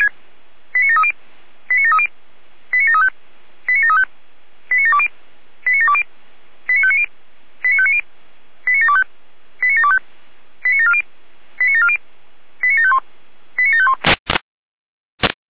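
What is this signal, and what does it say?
Five-tone selective call (ZVEI), heard as quick runs of five tones, tones of about 70 ms. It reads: 89730 89730 89735 89735 89720 89720 89590 89590 89725 89725 89740 89740 89751 89751
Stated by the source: SDR-received ZVEI five-tone selective call, code 89705.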